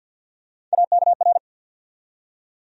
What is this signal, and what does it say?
Morse code, heard as a keyed tone, about 47 wpm, 700 Hz. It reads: AKR
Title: USA Call Sign Postfixes - 3 Letters 50wpm